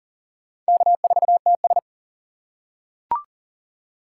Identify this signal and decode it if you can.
Morse code, heard as a keyed tone, about 40 wpm, 700 Hz. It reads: K4TS
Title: USA Call Signs - 4 Characters - 2x Character Spacing - 40wpm